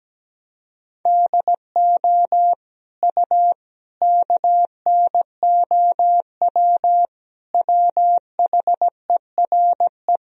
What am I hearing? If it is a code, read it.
Morse code, heard as a keyed tone, about 17 wpm, 700 Hz. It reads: DO U KNOW WHERE
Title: Sentences from Top 300 Words and Abbreviations 17wpm